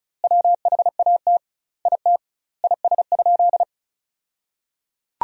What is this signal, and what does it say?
Morse code, heard as a keyed tone, about 36 wpm, 700 Hz. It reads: WHAT IT IS?
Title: 2x Word-Spacing - Sentences from Top 100 Words - 35wpm